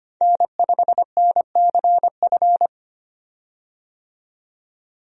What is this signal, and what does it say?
Morse code, heard as a keyed tone, about 25 wpm, 700 Hz. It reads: N5NCF
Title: USA Call Signs - 5 Characters - 25wpm